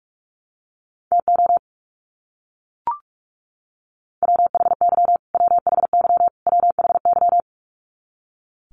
Morse code, heard as a keyed tone, about 44 wpm, 700 Hz. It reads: TO WHY WHY WHY
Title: Top 500 Words Repeated 45wpm